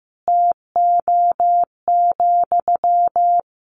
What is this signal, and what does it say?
Morse code, heard as a keyed tone, about 15 wpm, 700 Hz. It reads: TO,
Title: Sentences from Top 1000 Words 15wpm